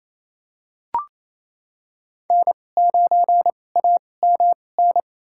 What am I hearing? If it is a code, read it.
Morse code, heard as a keyed tone, about 28 wpm, spaced as for 20 wpm, 700 Hz. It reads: N9AMN
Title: USA Call Signs - 5 Characters - 2x Character Spacing - 28wpm